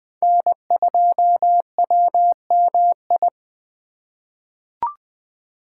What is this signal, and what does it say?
Morse code, heard as a keyed tone, about 20 wpm, 700 Hz. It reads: N2WMI
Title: USA Call Signs - 5 Characters - 3x Character Spacing - 20wpm